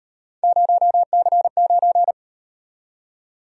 Morse code, heard as a keyed tone, about 38 wpm, 700 Hz. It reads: 0C9